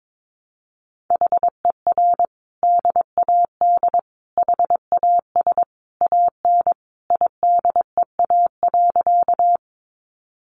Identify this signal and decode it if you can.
Morse code, heard as a keyed tone, about 22 wpm, 700 Hz. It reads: HER DAD HAS AN IDEA.
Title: Easy Phrases 22wpm